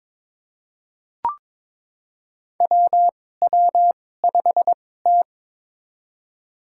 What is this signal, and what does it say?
Morse code, heard as a keyed tone, about 22 wpm, spaced as for 16 wpm, 700 Hz. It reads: WW5T
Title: USA Call Signs - 4 Characters - 2x Character Spacing - 22wpm